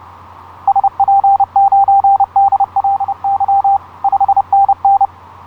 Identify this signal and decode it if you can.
Morse code, heard as a keyed tone, about 29 wpm, 790 Hz. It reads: SP9DLY 5NN
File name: zl1byz ha8wx etc 12m cw 06nov2010 0858